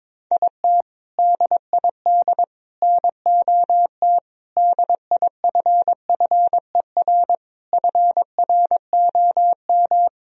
Morse code, heard as a keyed tone, about 22 wpm, 700 Hz. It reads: IT DID NOT DIFFER FROM
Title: Sentences from Top 1000 Words - Review Part 1 - 22wpm